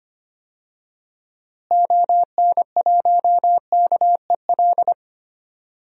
Morse code, heard as a keyed tone, about 25 wpm, 700 Hz. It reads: ON1KEL
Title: DXCC-100 Call Signs - 6 Characters - Repeated 3x - 25wpm